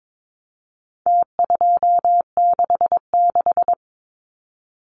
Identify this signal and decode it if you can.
Morse code, heard as a keyed tone, about 22 wpm, 700 Hz. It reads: T266
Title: Numbers - Contest Style - 4-Digit 22wpm